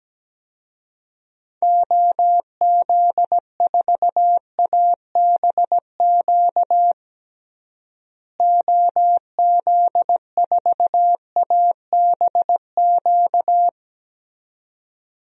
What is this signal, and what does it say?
Morse code, heard as a keyed tone, about 17 wpm, 700 Hz. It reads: OZ4ABQ OZ4ABQ
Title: DXCC-100 Call Signs - 6 Characters - Repeated 2x - 17wpm